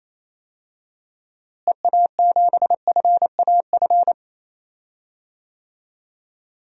Morse code, heard as a keyed tone, about 28 wpm, 700 Hz. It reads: EA7FAF
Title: DXCC-100 Call Signs - 6 Characters - Repeated 3x - 28wpm